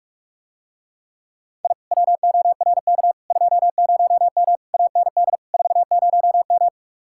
Morse code, heard as a keyed tone, about 44 wpm, 700 Hz. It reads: I WORK 20M AND 40M